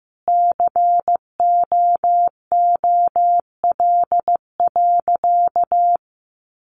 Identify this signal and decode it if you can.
Morse code, heard as a keyed tone, about 15 wpm, 700 Hz. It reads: COOL.
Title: Sentences from Top 1200 Words 15wpm